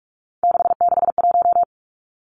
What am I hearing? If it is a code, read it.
Morse code, heard as a keyed tone, about 44 wpm, 700 Hz. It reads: 661
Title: Numbers - 3-Digit 45wpm